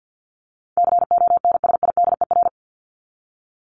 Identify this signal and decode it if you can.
Morse code, heard as a keyed tone, about 50 wpm, 700 Hz. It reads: CONSIDER
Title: ING Suffix - Top 100 Words with Base Words 50wpm